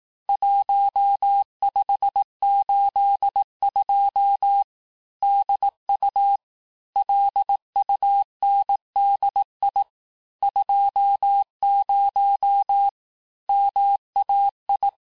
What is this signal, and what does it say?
Morse code, heard as a keyed tone, about 18 wpm, 800 Hz. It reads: 1582 DU LUNDI 20 MAI
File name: cours-du-lundi files / cours-1852-texte-chiffre-vitesse-17MPM